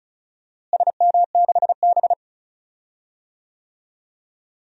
Morse code, heard as a keyed tone, about 34 wpm, 700 Hz. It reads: SM6B
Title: DXCC-100 Call Signs - 4 Characters - Repeated 2x - 35wpm